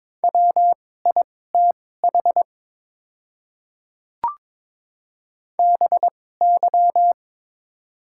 Morse code, heard as a keyed tone, about 22 wpm, spaced as for 16 wpm, 700 Hz. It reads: WITH BY
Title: 2x Character Spacing - Top 100 Words - 22wpm